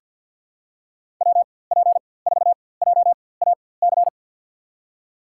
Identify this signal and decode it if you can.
Morse code, heard as a keyed tone, about 52 wpm, spaced as for 23 wpm, 700 Hz. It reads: WP4JAC
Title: USA Call Signs - 6 Characters - 4x Character Spacing - 50wpm